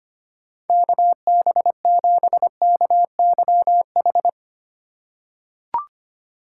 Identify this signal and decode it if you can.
Morse code, heard as a keyed tone, about 25 wpm, 700 Hz. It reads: KB7KYH
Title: USA Call Signs - 6 Characters - 2x Character Spacing - 25wpm